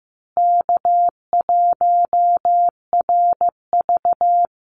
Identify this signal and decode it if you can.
Morse code, heard as a keyed tone, about 15 wpm, 700 Hz. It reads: K1RV
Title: Easy Call Signs Repeated 3x - 15wpm